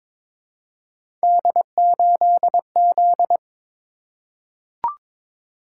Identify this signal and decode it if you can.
Morse code, heard as a keyed tone, about 22 wpm, 700 Hz. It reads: D8Z
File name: practice / 22wpm / Three Letter-Number Combinations - 22wpm